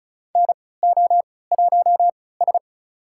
Morse code, heard as a keyed tone, about 36 wpm, spaced as for 20 wpm, 700 Hz. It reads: NO1S
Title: USA Call Signs - 4 Characters - 3x Character Spacing - 35wpm